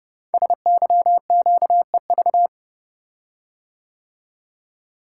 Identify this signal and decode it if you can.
Morse code, heard as a keyed tone, about 28 wpm, 690 Hz. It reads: SYQEV
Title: Code Groups - 5 Letters - 30wpm